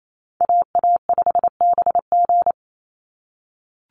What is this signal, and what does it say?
Morse code, heard as a keyed tone, about 28 wpm, 700 Hz. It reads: AA5BG
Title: Call Signs - 28wpm